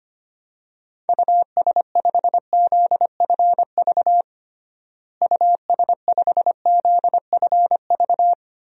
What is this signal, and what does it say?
Morse code, heard as a keyed tone, about 25 wpm, 700 Hz. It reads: US5ZFV US5ZFV
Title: DXCC-100 Call Signs - 6 Characters - Repeated 3x - 25wpm